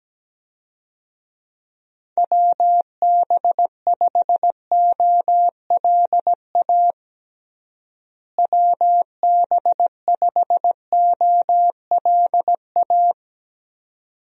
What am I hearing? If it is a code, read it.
Morse code, heard as a keyed tone, about 17 wpm, 700 Hz. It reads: WB5OLA WB5OLA
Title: USA Call Signs - 6 Characters - Repeated 2x - 17wpm